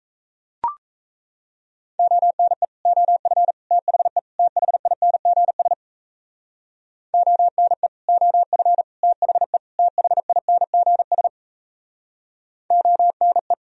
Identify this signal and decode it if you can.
Morse code, heard as a keyed tone, about 36 wpm, 700 Hz. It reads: ONE OF THE THINGS ONE OF THE THINGS ONE